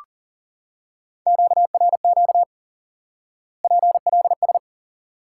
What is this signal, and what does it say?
Morse code, heard as a keyed tone, about 40 wpm, 700 Hz. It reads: QRQ PLS
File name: IQE Course 3 - Lesson 89 - YAGI - Two Elements - 40wpm